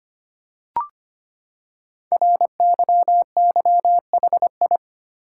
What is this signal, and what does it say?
Morse code, heard as a keyed tone, about 25 wpm, 700 Hz. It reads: RYYHI
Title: Code Groups - 5 Letters - 25wpm